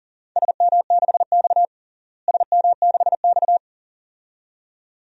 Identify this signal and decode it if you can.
Morse code, heard as a keyed tone, about 40 wpm, 700 Hz. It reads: SM6X SM6X
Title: DXCC-100 Call Signs - 4 Characters - Repeated 2x - 40wpm